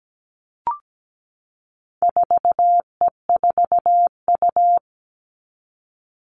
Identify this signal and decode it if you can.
Morse code, heard as a keyed tone, about 17 wpm, 700 Hz. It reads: 4E4U
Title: All Cut Numbers - 4-Digit 17wpm